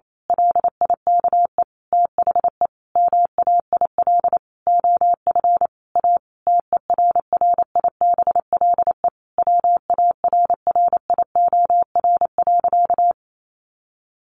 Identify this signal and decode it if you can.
Morse code, heard as a keyed tone, about 28 wpm, 700 Hz. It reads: LIKE THE MAIL OF A TERRIBLE WARRIOR.
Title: A Sound Of Thunder 28wpm